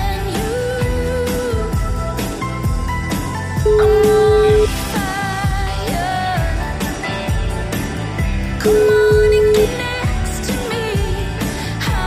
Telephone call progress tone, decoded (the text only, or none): ringback tone